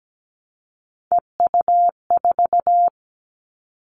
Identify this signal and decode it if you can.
Morse code, heard as a keyed tone, about 17 wpm, 700 Hz. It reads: EU4
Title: All Cut Numbers - 3-Digit 17wpm